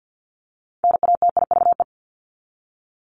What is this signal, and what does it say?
Morse code, heard as a keyed tone, about 50 wpm, 700 Hz. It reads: NATIVE
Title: Top 1751-2000 Words 50wpm